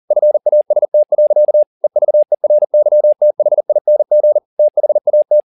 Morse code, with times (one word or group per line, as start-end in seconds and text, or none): 0.1-1.6 FAST.
1.8-4.4 EVERYTHING
4.6-5.4 THAT